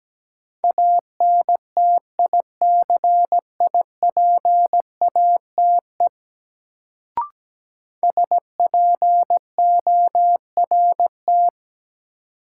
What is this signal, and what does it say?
Morse code, heard as a keyed tone, about 17 wpm, 700 Hz. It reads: ANTICIPATE SPORT
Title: Sentences from Top 1500 Words - New Word Drill Alt - 17wpm